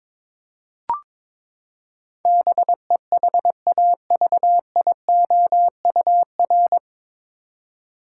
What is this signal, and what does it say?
Morse code, heard as a keyed tone, about 22 wpm, 700 Hz. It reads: BEHAVIOUR
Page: Morse Code Ninja practice 22wpm